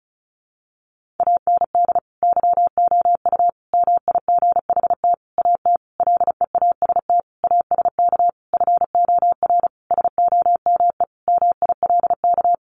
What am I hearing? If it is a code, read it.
Morse code, heard as a keyed tone, about 34 wpm, 700 Hz. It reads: AND YOU MIGHT AT LEAST ASK FOR SOME MILK
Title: Sentences from Top 1500 Words 35wpm